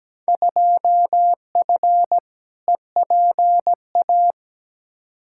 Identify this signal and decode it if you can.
Morse code, heard as a keyed tone, about 17 wpm, 700 Hz. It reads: 2F EPA
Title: Field Day Exchanges - 17wpm